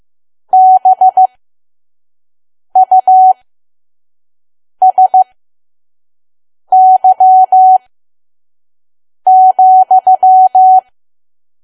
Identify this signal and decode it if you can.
Morse code, heard as a keyed tone, about 15 wpm, spaced as for 5 wpm, 750 Hz. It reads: BUSY,
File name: Archive / 5 WPM files / PM